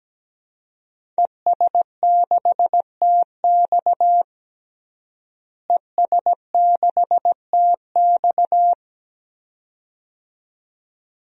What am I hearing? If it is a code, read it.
Morse code, heard as a keyed tone, about 17 wpm, 700 Hz. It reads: ES6TX ES6TX
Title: DXCC-100 Call Signs - 5 Characters - Repeated 2x - 17wpm